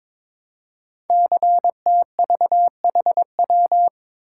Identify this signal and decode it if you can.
Morse code, heard as a keyed tone, about 22 wpm, 700 Hz. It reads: CTVHW
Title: Code Groups - 5 Letters - 22wpm